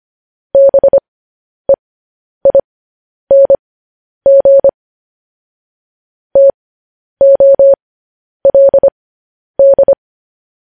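Morse code, heard as a keyed tone, about 25 wpm, spaced as for 10 wpm, 550 Hz. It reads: BEING TOLD
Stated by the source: Morse code news headlines.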